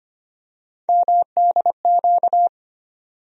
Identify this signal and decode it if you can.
Morse code, heard as a keyed tone, about 25 wpm, 700 Hz. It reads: MDQ
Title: USA Call Sign Postfixes - 3 Letters 25wpm